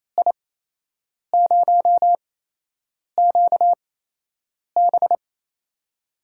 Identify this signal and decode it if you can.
Morse code, heard as a keyed tone, about 28 wpm, spaced as for 8 wpm, 700 Hz. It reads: I0QB
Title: USA Call Signs - 5 Characters - 8x Character Spacing - 28wpm